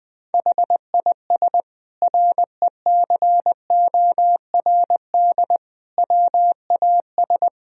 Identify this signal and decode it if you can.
Morse code, heard as a keyed tone, about 20 wpm, 700 Hz. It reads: HIS RECORD WAS
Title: Sentences from Top 1000 Words - Review Part 3 - 20wpm